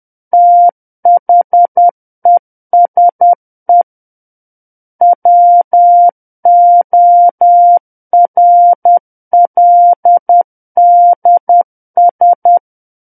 Morse code, heard as a keyed tone, about 10 wpm, 700 Hz. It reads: THESE WORLDS
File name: Book One / 01-Chapter 1 - 10 WPM